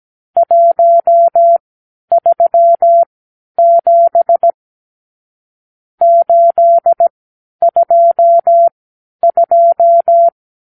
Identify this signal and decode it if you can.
Morse code, heard as a keyed tone, about 17 wpm, spaced as for 10 wpm, 680 Hz. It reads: 137 822
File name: arrl / Review (1-0)